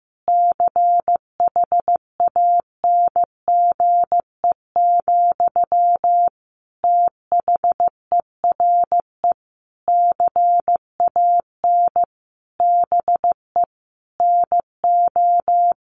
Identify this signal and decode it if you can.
Morse code, heard as a keyed tone, about 15 wpm, 700 Hz. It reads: CHANGE, THERE CAN BE NO
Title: Sentences from Top 1400 Words 15wpm